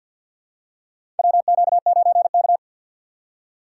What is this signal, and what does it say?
Morse code, heard as a keyed tone, about 53 wpm, 700 Hz. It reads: WQ9K
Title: USA Call Signs - 4 Characters - 6x Character Spacing - 50wpm